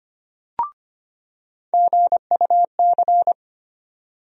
Morse code, heard as a keyed tone, about 25 wpm, 700 Hz. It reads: GUC
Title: USA Call Sign Postfixes - 3 Letters 25wpm